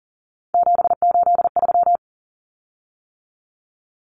Morse code, heard as a keyed tone, about 38 wpm, 700 Hz. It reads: 783
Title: Common Cut Numbers - 3-Digit 40wpm